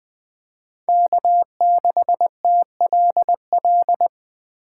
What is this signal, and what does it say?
Morse code, heard as a keyed tone, about 21 wpm, 700 Hz. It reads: K6TLL